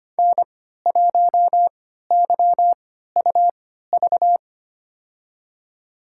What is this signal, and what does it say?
Morse code, heard as a keyed tone, about 25 wpm, spaced as for 14 wpm, 700 Hz. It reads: N1YUV
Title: USA Call Signs - 3x Character Spacing - 25wpm